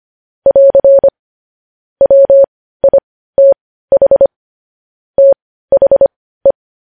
Morse code, heard as a keyed tone, about 25 wpm, spaced as for 15 wpm, 550 Hz. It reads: <AR> WITH THE
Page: news headlines in morse code at 15 wpm